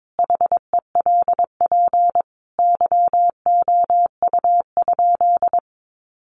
Morse code, heard as a keyed tone, about 22 wpm, 700 Hz. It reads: HELP YOU?
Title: Sentences from Top 200 Words 22wpm